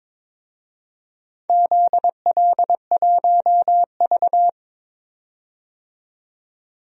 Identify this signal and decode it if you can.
Morse code, heard as a keyed tone, about 22 wpm, 690 Hz. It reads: ZL1V
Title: DXCC-100 Call Signs - 4 Characters - 22wpm